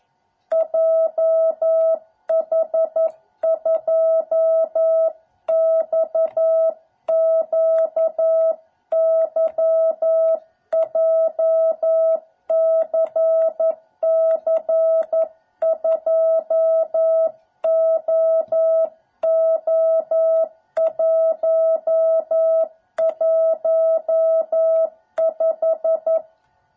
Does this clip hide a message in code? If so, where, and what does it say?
Yes: Morse code from 0.5 to 26.2 s, heard as a keyed tone, about 11 wpm, 660 Hz. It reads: JH2XQY JCC2OO115